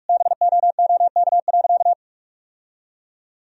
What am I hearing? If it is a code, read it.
Morse code, heard as a keyed tone, about 46 wpm, 700 Hz. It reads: BOOK.
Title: Sentences from Top 1000 Words - Review Part 1 - 45wpm